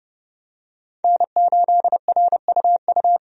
Morse code, heard as a keyed tone, about 32 wpm, 700 Hz. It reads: N8RUU
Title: USA Call Signs - 5 Characters - 30wpm